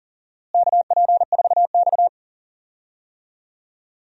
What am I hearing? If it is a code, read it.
Morse code, heard as a keyed tone, about 40 wpm, 700 Hz. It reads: KP4X
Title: DXCC-100 Call Signs - 4 Characters - Repeated 3x - 40wpm